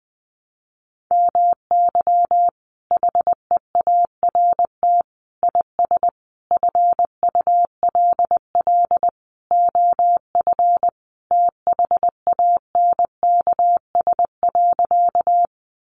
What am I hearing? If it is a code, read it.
Morse code, heard as a keyed tone, about 19 wpm, 700 Hz. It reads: MY HEART IS FULL OF THANKS.
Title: Sentences from Top 1200 Words 20wpm